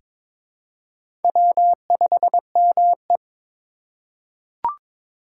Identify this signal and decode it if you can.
Morse code, heard as a keyed tone, about 22 wpm, 700 Hz. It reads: W5ME
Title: USA Call Signs - 4 Characters - 3x Character Spacing - 22wpm